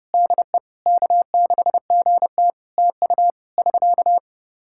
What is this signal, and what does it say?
Morse code, heard as a keyed tone, about 30 wpm, 700 Hz. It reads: DE K6GT TU <SK>
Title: Practice QSO - Exchange 4 - Ending - 30wpm